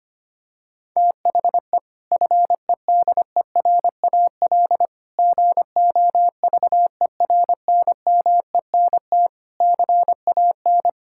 Morse code, heard as a keyed tone, about 25 wpm, 700 Hz. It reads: THE FEDERAL GOVERNMENT CAN
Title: Sentences from Top 1000 Words - Review Part 5 - 25wpm